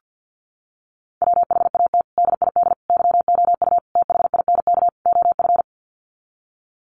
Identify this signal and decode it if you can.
Morse code, heard as a keyed tone, about 50 wpm, 700 Hz. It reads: WHAT DID YOU THINK OF
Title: Sets of 5 Words Encore2 50wpm